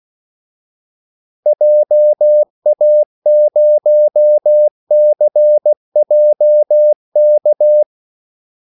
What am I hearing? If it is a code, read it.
Morse code, heard as a keyed tone, about 16 wpm, 600 Hz. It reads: JA0CJK
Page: Kaldesignaler 16 wpm